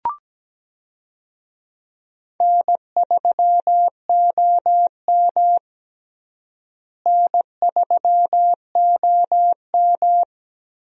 Morse code, heard as a keyed tone, about 17 wpm, 700 Hz. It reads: N3OM N3OM
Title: USA Call Signs - 4 Characters - Repeated 3x - 17wpm